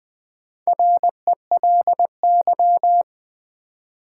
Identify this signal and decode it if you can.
Morse code, heard as a keyed tone, about 20 wpm, 700 Hz. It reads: RELY
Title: Sentences from Top 1200 Words - New Word Drill Alt - 20wpm